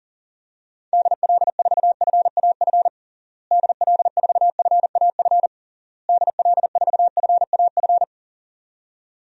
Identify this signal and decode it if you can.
Morse code, heard as a keyed tone, about 40 wpm, 700 Hz. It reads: DL4FAF DL4FAF DL4FAF
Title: DXCC-100 Call Signs - 6 Characters - Repeated 3x - 40wpm